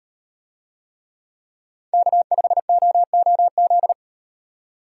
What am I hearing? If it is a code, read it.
Morse code, heard as a keyed tone, about 38 wpm, 700 Hz. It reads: K5OOZ